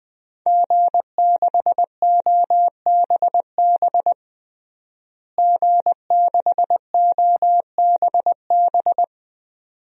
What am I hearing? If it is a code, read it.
Morse code, heard as a keyed tone, about 20 wpm, 700 Hz. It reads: G6OBB G6OBB
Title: DXCC-100 Call Signs - 5 Characters - Repeated 2x - 20wpm